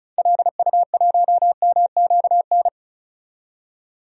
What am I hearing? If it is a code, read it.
Morse code, heard as a keyed tone, about 34 wpm, 700 Hz. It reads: LU1MQN
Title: DXCC-100 Call Signs - 6 Characters - 35wpm